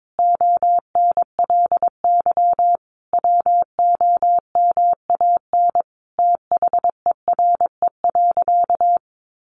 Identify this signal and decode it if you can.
Morse code, heard as a keyed tone, about 22 wpm, 700 Hz. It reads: ONLY WOMAN THERE.
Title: Sentences from Top 1100 Words 22wpm